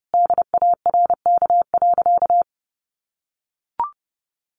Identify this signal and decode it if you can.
Morse code, heard as a keyed tone, about 30 wpm, 700 Hz. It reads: DARK.
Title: Sentences from Top 600 Words 30wpm